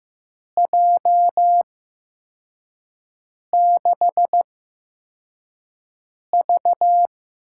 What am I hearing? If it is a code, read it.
Morse code, heard as a keyed tone, about 15 wpm, spaced as for 4 wpm, 700 Hz. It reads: J6V
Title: USA Call Signs - 6 Characters - 8x Character Spacing - 15wpm